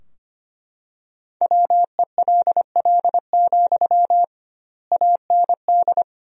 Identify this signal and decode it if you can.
Morse code, heard as a keyed tone, about 25 wpm, 700 Hz. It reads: WELL, AND
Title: 2x Word-Spacing - Sentences from Top 200 Words - 25wpm